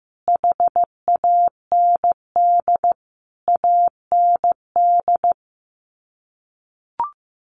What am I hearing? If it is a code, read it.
Morse code, heard as a keyed tone, about 15 wpm, 700 Hz. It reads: HAND AND
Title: Sets of 5 Words Encore2 15wpm